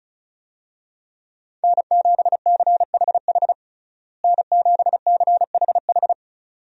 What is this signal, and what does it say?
Morse code, heard as a keyed tone, about 34 wpm, 700 Hz. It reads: N7CHH N7CHH
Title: USA Call Signs - 5 Characters - Repeated 2x - 35wpm